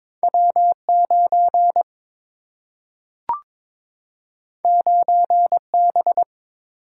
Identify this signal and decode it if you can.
Morse code, heard as a keyed tone, about 22 wpm, 700 Hz. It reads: W 9 9 B
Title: Single-Letter Single-Number 22wpm